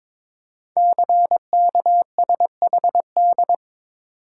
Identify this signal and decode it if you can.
Morse code, heard as a keyed tone, about 22 wpm, 700 Hz. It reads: CKSHD
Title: Code Groups - 5 Letters - 22wpm